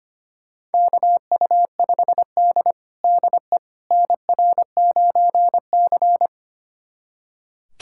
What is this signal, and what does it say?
Morse code, heard as a keyed tone, about 25 wpm, 700 Hz. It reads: KU5D DE NR9C
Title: USA Callsign DE Callsign - 4 Characters - 25wpm